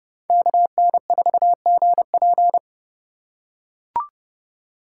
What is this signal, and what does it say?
Morse code, heard as a keyed tone, about 30 wpm, 700 Hz. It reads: KN4GP
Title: USA Call Signs - 5 Characters - 3x Character Spacing - 30wpm